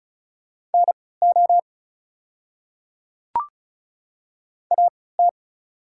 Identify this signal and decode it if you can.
Morse code, heard as a keyed tone, about 36 wpm, spaced as for 20 wpm, 690 Hz. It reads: NO AT
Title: 3x Character Spacing - Top 100 Words - 35wpm